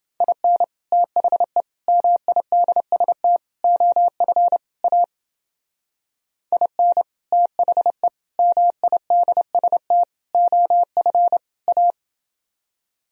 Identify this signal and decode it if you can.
Morse code, heard as a keyed tone, about 29 wpm, 700 Hz. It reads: IN THE MIDST OF A IN THE MIDST OF A